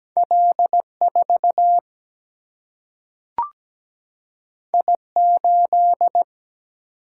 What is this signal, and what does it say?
Morse code, heard as a keyed tone, about 17 wpm, 700 Hz. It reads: L4 I8
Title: Single-Letter Single-Number 17wpm